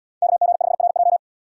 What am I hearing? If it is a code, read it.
Morse code, heard as a keyed tone, about 74 wpm, 700 Hz. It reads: KG5RJ